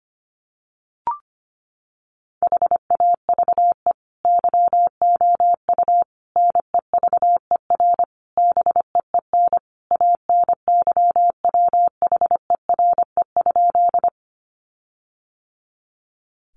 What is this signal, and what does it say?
Morse code, heard as a keyed tone, about 25 wpm, 700 Hz. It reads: HAVE YOU NEVER BEEN ANYWHERE?
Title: Sentences from Top 1400 Words 25wpm